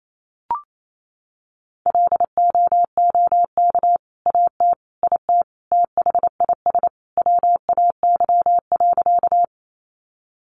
Morse code, heard as a keyed tone, about 28 wpm, 700 Hz. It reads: LOOK AT IT THIS WAY.
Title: Sentences from Top 100 Words 28wpm